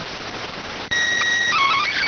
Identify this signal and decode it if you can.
SSTV (slow-scan television), PD 290.